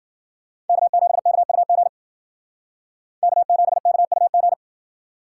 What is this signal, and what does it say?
Morse code, heard as a keyed tone, about 50 wpm, 700 Hz. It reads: K7KUG K7KUG
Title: Speed Racing - USA 1x3 Call Signs 40wpm